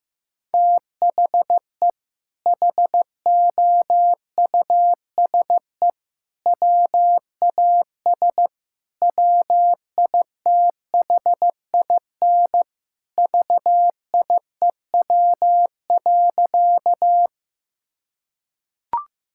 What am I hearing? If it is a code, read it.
Morse code, heard as a keyed tone, about 15 wpm, 700 Hz. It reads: THE HOUSE WAS WITHIN VIEW.
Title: Sentences from Top 1000 Words - Review Part 6 - 15wpm